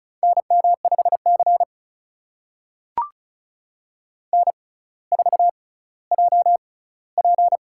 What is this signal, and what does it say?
Morse code, heard as a keyed tone, about 36 wpm, spaced as for 18 wpm, 700 Hz. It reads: NM5C N4JP
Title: USA Call Signs - 4 Characters - 6x Character Spacing - 35wpm